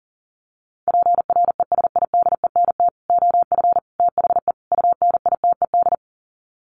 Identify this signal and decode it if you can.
Morse code, heard as a keyed tone, about 40 wpm, 700 Hz. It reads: PRESIDENT OF THE UNITED
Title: Sets of 4 Words 40wpm